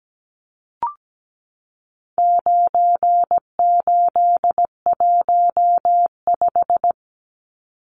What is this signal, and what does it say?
Morse code, heard as a keyed tone, about 17 wpm, 700 Hz. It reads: 9815
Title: Numbers - 4-Digit 17wpm